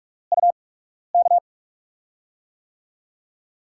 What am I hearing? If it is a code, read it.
Morse code, heard as a keyed tone, about 44 wpm, spaced as for 12 wpm, 700 Hz. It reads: UK
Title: USA Call Signs - 6 Characters - 8x Character Spacing - 45wpm